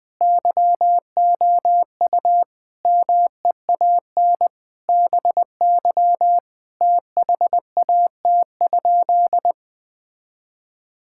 Morse code, heard as a keyed tone, about 20 wpm, 700 Hz. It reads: YOU MEAN BY THAT?
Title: Sentences with 4-Word Phrases - Spoken Beginning - 20wpm